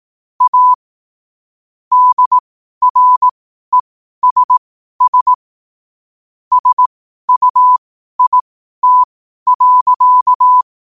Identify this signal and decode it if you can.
Morse code, heard as a keyed tone, about 18 wpm, spaced as for 12 wpm, 1000 Hz. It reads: A DRESS SUIT.
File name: qotdfiles / 12wpm